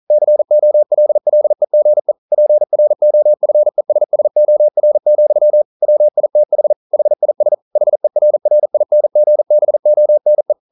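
Morse code, heard as a keyed tone, about 41 wpm, 600 Hz. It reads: COLLEGE PROFESSOR, WITH HIS HERRINGBONE
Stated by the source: Morse code practice at 41 wpm, from CW med Gnister.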